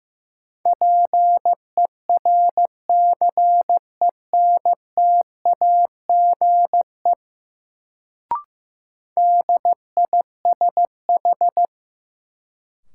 Morse code, heard as a keyed tone, about 15 wpm, 700 Hz. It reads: PERCENTAGE DISH
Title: Sentences from Top 1300 Words - New Word Drill Alt - 15wpm